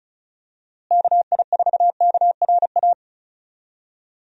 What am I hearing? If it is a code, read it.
Morse code, heard as a keyed tone, about 36 wpm, 700 Hz. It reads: KI4KRA
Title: USA Call Signs - 6 Characters - Repeated 3x - 35wpm